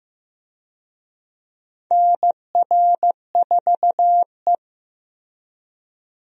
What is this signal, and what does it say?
Morse code, heard as a keyed tone, about 15 wpm, 700 Hz. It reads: NR4E
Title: USA Call Signs - 4 Characters - Repeated 3x - 15wpm